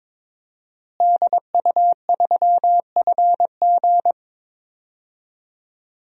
Morse code, heard as a keyed tone, about 22 wpm, 700 Hz. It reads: DU3FG
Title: DXCC-100 Call Signs - 5 Characters - Repeated 2x - 22wpm